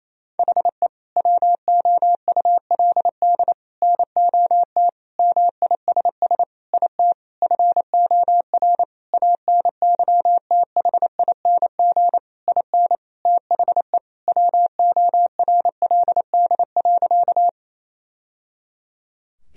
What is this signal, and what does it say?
Morse code, heard as a keyed tone, about 27 wpm, 700 Hz. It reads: HE WOULD NOT MISS IT FOR ANYTHING IN THE WORLD.
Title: Sentences from Top 1000 Words - Review Part 3 - 28wpm